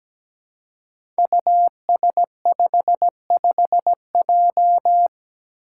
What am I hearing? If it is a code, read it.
Morse code, heard as a keyed tone, about 17 wpm, 700 Hz. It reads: US55J
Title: Five Challenging Letter-Number Combinations 17wpm